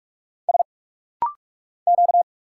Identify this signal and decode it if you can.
Morse code, heard as a keyed tone, about 44 wpm, 700 Hz. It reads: S Q
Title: Single Letter-Number Alt - Rapid-Fire 45wpm